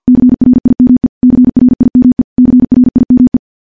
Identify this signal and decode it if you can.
Morse code, heard as a keyed tone, about 50 wpm, 250 Hz. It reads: CRIME CRIME CRIME
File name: morse / morse-training-50WPM